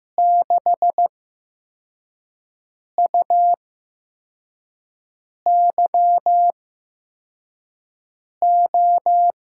Morse code, heard as a keyed tone, about 15 wpm, spaced as for 4 wpm, 700 Hz. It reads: 6UYO
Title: USA Call Signs - 8x Character Spacing - 15wpm